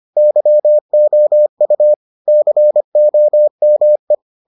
Morse code, heard as a keyed tone, about 25 wpm, 600 Hz. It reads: YOU COME